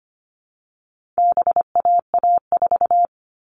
Morse code, heard as a keyed tone, about 25 wpm, 700 Hz. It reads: BAA4
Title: All Cut Numbers - 4-Digit 25wpm